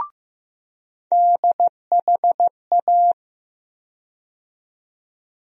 Morse code, heard as a keyed tone, about 15 wpm, 700 Hz. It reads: DHA
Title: USA Call Sign Postfixes - 3 Letters 15wpm